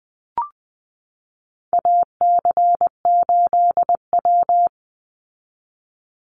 Morse code, heard as a keyed tone, about 20 wpm, 700 Hz. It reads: AC8W
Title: Call Signs Repeated 3x - 20wpm